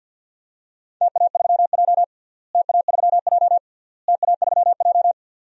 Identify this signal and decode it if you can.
Morse code, heard as a keyed tone, about 51 wpm, 700 Hz. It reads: TA3J TA3J TA3J
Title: DXCC-100 Call Signs - 4 Characters - Repeated 3x - 50wpm